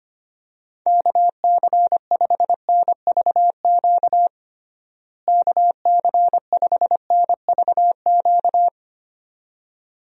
Morse code, heard as a keyed tone, about 25 wpm, 700 Hz. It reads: KC5NVQ KC5NVQ
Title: USA Call Signs - 6 Characters - Repeated 2x - 25wpm